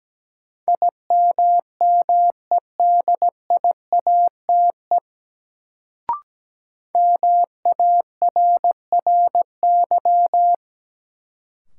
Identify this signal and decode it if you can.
Morse code, heard as a keyed tone, about 17 wpm, 700 Hz. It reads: IMMEDIATE MARRY
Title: Sentences from Top 1200 Words - New Word Drill - 17wpm